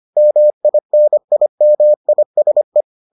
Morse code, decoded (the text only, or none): MINIMISE